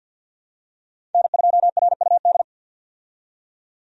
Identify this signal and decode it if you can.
Morse code, heard as a keyed tone, about 49 wpm, 700 Hz. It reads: N2RUD
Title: USA Call Signs - 5 Characters - 50wpm